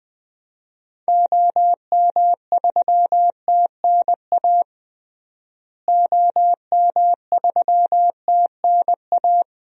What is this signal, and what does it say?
Morse code, heard as a keyed tone, about 20 wpm, 700 Hz. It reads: OM3TNA OM3TNA
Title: DXCC-100 Call Signs - 6 Characters - Repeated 3x - 20wpm